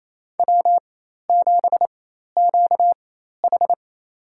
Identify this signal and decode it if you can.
Morse code, heard as a keyed tone, about 28 wpm, spaced as for 13 wpm, 700 Hz. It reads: W7QH
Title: USA Call Signs - 4 Characters - 4x Character Spacing - 28wpm